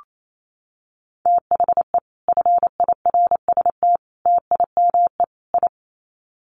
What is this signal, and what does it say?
Morse code, heard as a keyed tone, about 28 wpm, 700 Hz. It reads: THE FIRST TIME I
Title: Sets of 4 Words 28wpm